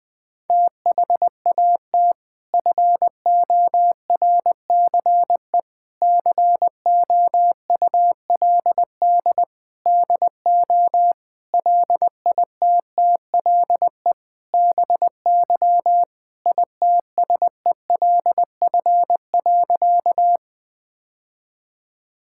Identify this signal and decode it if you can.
Morse code, heard as a keyed tone, about 20 wpm, 700 Hz. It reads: THAT FORCE COULD DO LITTLE BY ITSELF.
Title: Sentences from Top 1000 Words - Review Part 5 - 20wpm